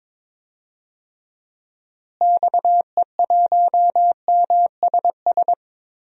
Morse code, heard as a keyed tone, about 22 wpm, 700 Hz. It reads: XE1MSS
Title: DXCC-100 Call Signs - 6 Characters - Repeated 3x - 22wpm